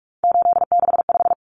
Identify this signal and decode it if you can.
Morse code, heard as a keyed tone, about 44 wpm, 700 Hz. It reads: 865
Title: Common Cut Numbers - 3-Digit 45wpm